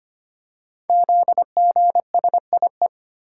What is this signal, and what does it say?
Morse code, heard as a keyed tone, about 25 wpm, 700 Hz. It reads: ZGSIE
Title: Code Groups - 5 Letters - 25wpm